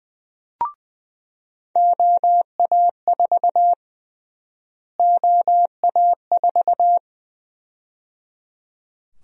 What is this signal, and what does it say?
Morse code, heard as a keyed tone, about 20 wpm, 700 Hz. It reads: OA4 OA4
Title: USA Call Sign Prefixes - Repeated 2x - 20wpm